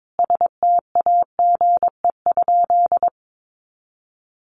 Morse code, heard as a keyed tone, about 22 wpm, 700 Hz. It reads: STAGE?
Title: Sentences from Top 1200 Words 22wpm